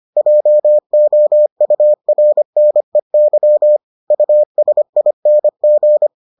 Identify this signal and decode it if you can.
Morse code, heard as a keyed tone, about 25 wpm, 600 Hz. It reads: JOURNEY USING